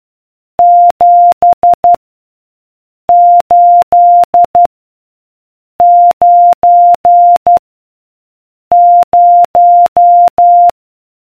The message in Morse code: 7 8 9 0